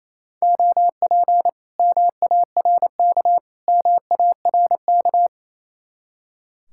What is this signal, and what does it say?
Morse code, heard as a keyed tone, about 37 wpm, 700 Hz. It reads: OP MARK MARK